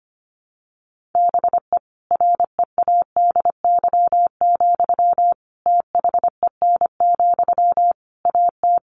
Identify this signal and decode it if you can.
Morse code, heard as a keyed tone, about 25 wpm, 700 Hz. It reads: BE READY, THEN, AT
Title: Sentences from Top 1400 Words 25wpm